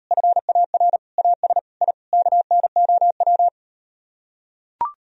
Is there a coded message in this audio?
Yes: Morse code from 0.1 to 3.5 s, heard as a keyed tone, about 38 wpm, 700 Hz. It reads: FAR AS I KNOW